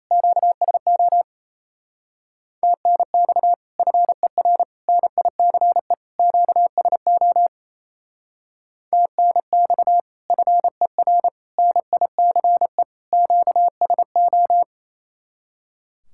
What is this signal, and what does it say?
Morse code, heard as a keyed tone, about 37 wpm, 700 Hz. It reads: QSO TNX FER NICE QSO TNX FER NICE QSO